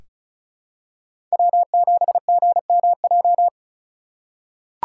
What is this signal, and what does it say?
Morse code, heard as a keyed tone, about 36 wpm, 700 Hz. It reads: W7GMJ
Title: USA Call Signs - 5 Characters - 3x Character Spacing - 35wpm